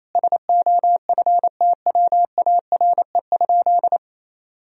Morse code, heard as a keyed tone, about 28 wpm, 700 Hz. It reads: SOFTWARE?
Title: Sentences from Top 1000 Words - Review Part 3 - 28wpm